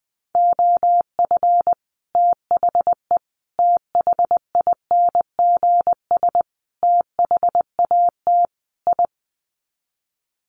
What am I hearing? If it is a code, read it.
Morse code, heard as a keyed tone, about 20 wpm, 700 Hz. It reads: OF THE THINGS THAT I
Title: Sets of 5 Words Encore 20wpm